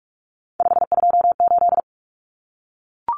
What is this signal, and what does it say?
Morse code, heard as a keyed tone, about 44 wpm, 700 Hz. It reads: 528
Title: Numbers - 3-Digit 45wpm